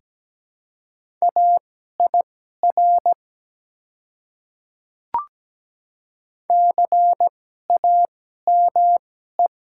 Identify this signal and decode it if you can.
Morse code, heard as a keyed tone, about 17 wpm, spaced as for 12 wpm, 700 Hz. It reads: AIR CAME
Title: 2x Character Spacing - Top 200 Words - 17wpm